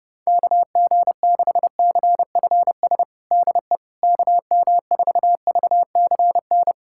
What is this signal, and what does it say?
Morse code, heard as a keyed tone, about 30 wpm, 700 Hz. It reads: KG6CFS DE KM4VCN
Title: USA Callsign DE Callsign - 6 Characters - 30wpm